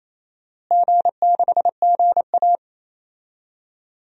Morse code, heard as a keyed tone, about 28 wpm, 700 Hz. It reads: G6GA
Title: DXCC-100 Call Signs - 4 Characters - Repeated 2x - 28wpm